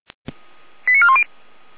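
Five-tone selective call (ZVEI), heard as quick runs of five tones, tones of about 70 ms. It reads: 89410
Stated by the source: Five-tone ZVEI selective call received on an SDR.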